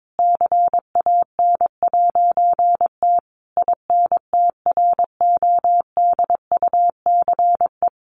Morse code, heard as a keyed tone, about 22 wpm, 700 Hz. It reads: CAN'T INTRODUCE
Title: A Sound Of Thunder 22wpm